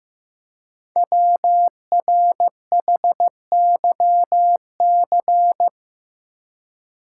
Morse code, heard as a keyed tone, about 15 wpm, 700 Hz. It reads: WRHYC
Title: Code Groups - 5 Letters - 15wpm